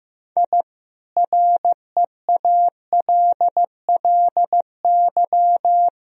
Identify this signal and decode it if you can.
Morse code, heard as a keyed tone, about 15 wpm, 700 Hz. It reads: I REALLY
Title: Sentences from Top 1000 Words - Review Part 4 - 15wpm